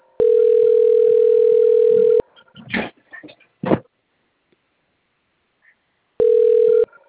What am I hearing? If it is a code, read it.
Ringback tone (North America), 440 + 480 Hz.
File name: rg-1600-unknown- - - .91157